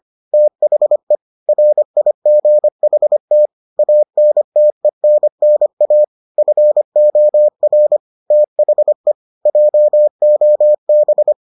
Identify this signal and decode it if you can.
Morse code, heard as a keyed tone, about 25 wpm, 600 Hz. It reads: THE RIGHT ANTENNA FOR THE JOB